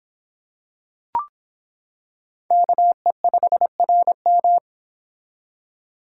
Morse code, heard as a keyed tone, about 24 wpm, 700 Hz. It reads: KE5RM